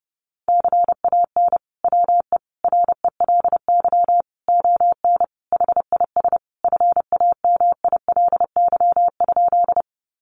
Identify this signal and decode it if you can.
Morse code, heard as a keyed tone, about 30 wpm, 700 Hz. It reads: CAN WE RELY ON HIS FAMILY?
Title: Sentences from Top 1200 Words 30wpm